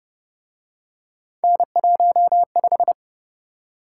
Morse code, heard as a keyed tone, about 29 wpm, 700 Hz. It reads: N15